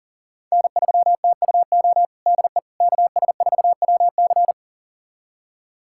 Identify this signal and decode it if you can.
Morse code, heard as a keyed tone, about 40 wpm, 700 Hz. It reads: N3TUO DE KS4WC
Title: USA Callsign DE Callsign - 5 Characters - 40wpm